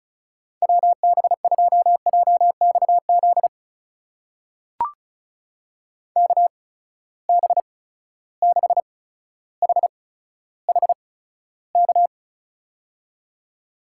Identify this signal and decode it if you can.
Morse code, heard as a keyed tone, about 36 wpm, spaced as for 15 wpm, 700 Hz. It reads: WB2JXZ KB6HHK
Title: USA Call Signs - 6 Characters - 8x Character Spacing - 35wpm